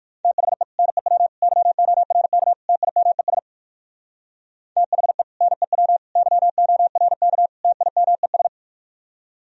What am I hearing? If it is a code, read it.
Morse code, heard as a keyed tone, about 58 wpm, 700 Hz. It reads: THE NEW YORK TIMES THE NEW YORK TIMES